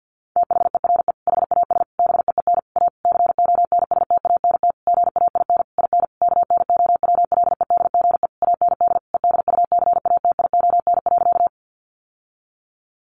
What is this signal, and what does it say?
Morse code, heard as a keyed tone, about 50 wpm, 700 Hz. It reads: THERE HAS BEEN A CONSTANT GAIN IN KNOWLEDGE AND EDUCATION.
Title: Sentences from Top 1500 Words 50wpm